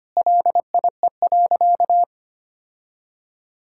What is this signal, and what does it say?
Morse code, heard as a keyed tone, about 25 wpm, 700 Hz. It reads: LIE.
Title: Sentences from Top 1600 Words 25wpm